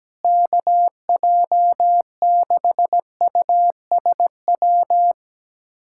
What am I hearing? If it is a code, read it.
Morse code, heard as a keyed tone, about 17 wpm, 700 Hz. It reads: KJ6USW
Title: USA Callsign DE Callsign - 6 Characters - 17wpm